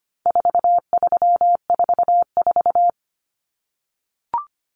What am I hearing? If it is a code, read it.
Morse code, heard as a keyed tone, about 25 wpm, 700 Hz. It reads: 4344
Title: Common Cut Numbers - 4-Digit 25wpm